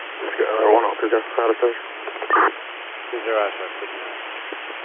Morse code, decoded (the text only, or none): none